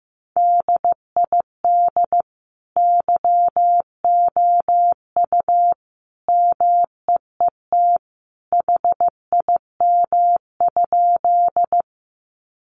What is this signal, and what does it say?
Morse code, heard as a keyed tone, about 14 wpm, 700 Hz. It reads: DID YOU MEET HIM?
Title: Sentences from Top 1100 Words 15wpm